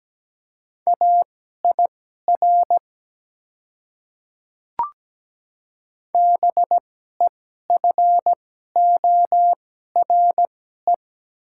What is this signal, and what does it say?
Morse code, heard as a keyed tone, about 17 wpm, spaced as for 12 wpm, 700 Hz. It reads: AIR BEFORE
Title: 2x Character Spacing - Top 300 Words - 17wpm